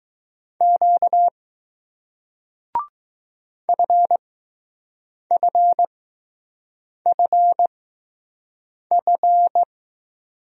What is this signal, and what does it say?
Morse code, heard as a keyed tone, about 20 wpm, spaced as for 6 wpm, 700 Hz. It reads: Q FFFF